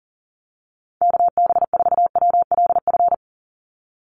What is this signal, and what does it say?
Morse code, heard as a keyed tone, about 40 wpm, 700 Hz. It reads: KB4WLF
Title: Call Signs Repeated 3x - 40wpm